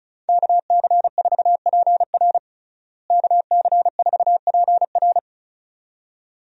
Morse code, heard as a keyed tone, about 36 wpm, 700 Hz. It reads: KC4PR KC4PR
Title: USA Call Signs - 5 Characters - Repeated 2x - 35wpm